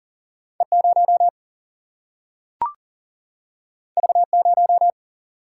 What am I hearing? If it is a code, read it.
Morse code, heard as a keyed tone, about 40 wpm, 700 Hz. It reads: E 0 V 0
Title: Single-Letter Single-Number 40wpm